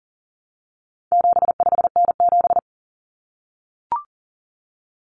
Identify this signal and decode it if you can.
Morse code, heard as a keyed tone, about 40 wpm, 700 Hz. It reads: 75N7
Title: Common Cut Numbers - 4-Digit 40wpm